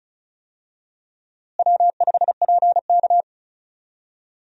Morse code, heard as a keyed tone, about 36 wpm, 700 Hz. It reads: W5PK
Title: USA Call Signs - 4 Characters - Repeated 2x - 35wpm